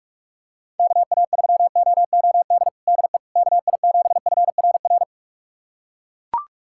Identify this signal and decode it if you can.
Morse code, heard as a keyed tone, about 41 wpm, 700 Hz. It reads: KA3OOD DE KI7FRR
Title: USA Callsign DE Callsign 45wpm